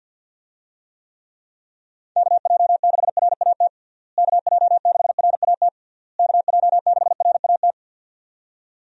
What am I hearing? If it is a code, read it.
Morse code, heard as a keyed tone, about 52 wpm, 700 Hz. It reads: KJ6RAT KJ6RAT KJ6RAT
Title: USA Call Signs - 6 Characters - Repeated 3x - 50wpm